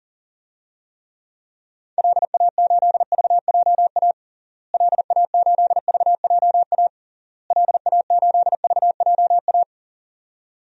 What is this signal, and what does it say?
Morse code, heard as a keyed tone, about 37 wpm, 700 Hz. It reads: LA8VJA LA8VJA LA8VJA
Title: DXCC-100 Call Signs - 6 Characters - Repeated 3x - 40wpm